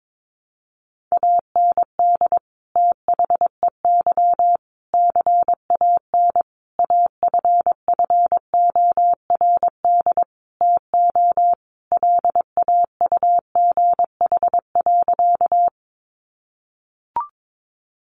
Morse code, heard as a keyed tone, about 22 wpm, 700 Hz. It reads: AND THEY CAN AFFORD TO LAUGH.
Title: Sentences from Top 1100 Words 22wpm